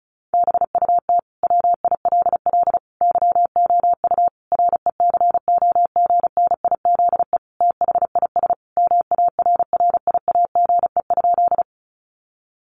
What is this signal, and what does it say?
Morse code, heard as a keyed tone, about 34 wpm, 700 Hz. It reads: BUT WILL YOU RECOGNIZE THIS MARRIAGE?
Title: Sentences from Top 1400 Words 35wpm